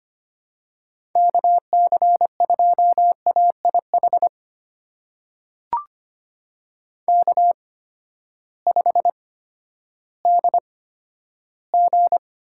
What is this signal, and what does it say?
Morse code, heard as a keyed tone, about 25 wpm, spaced as for 12 wpm, 700 Hz. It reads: KC2AIH K5DG
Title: USA Call Signs - 8x Character Spacing - 25wpm